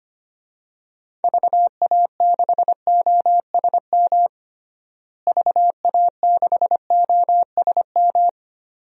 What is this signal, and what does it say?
Morse code, heard as a keyed tone, about 25 wpm, 700 Hz. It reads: VA6OSM VA6OSM
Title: DXCC-100 Call Signs - 6 Characters - Repeated 2x - 25wpm